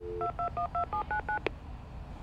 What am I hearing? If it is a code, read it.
DTMF, heard as telephone touch tones, tones of about 110 ms. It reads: 2212799